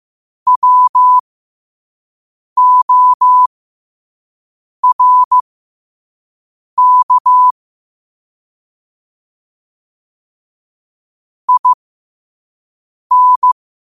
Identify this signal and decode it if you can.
Morse code, heard as a keyed tone, about 15 wpm, spaced as for 5 wpm, 1000 Hz. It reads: WORK IN
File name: qotdfiles / 5wpm